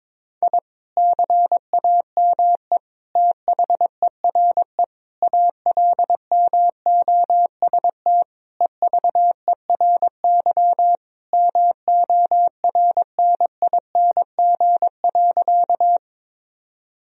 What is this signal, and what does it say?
Morse code, heard as a keyed tone, about 22 wpm, 700 Hz. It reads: I CAME THERE ALMOST EVERY MORNING.
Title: Sentences from Top 1000 Words - Review Part 4 - 22wpm